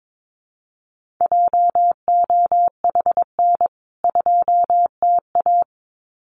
Morse code, heard as a keyed tone, about 22 wpm, 700 Hz. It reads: JOHN 2TA
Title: CWT Exchange 22wpm